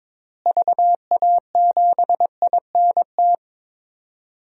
Morse code, heard as a keyed tone, about 22 wpm, 700 Hz. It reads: VA7INT
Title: DXCC-100 Call Signs - 6 Characters - Repeated 3x - 22wpm